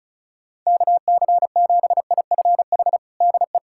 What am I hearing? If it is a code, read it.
Morse code, heard as a keyed tone, about 34 wpm, 700 Hz. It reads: KC7IFH DE
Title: USA Callsign DE Callsign 35wpm